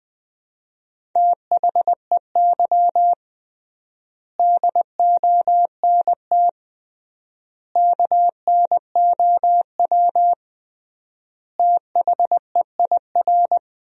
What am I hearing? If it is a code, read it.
Morse code, heard as a keyed tone, about 20 wpm, 700 Hz. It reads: THEY DONT KNOW THEIR
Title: 3x Word-Spacing - Sentences from Top 300 Words - 20wpm